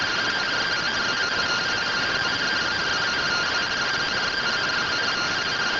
Two-tone teleprinter signal, about 170 Hz shift.